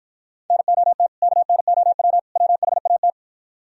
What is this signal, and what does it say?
Morse code, heard as a keyed tone, about 52 wpm, 690 Hz. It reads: NOT KNOW WHAT